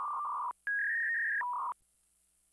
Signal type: fax machine handshake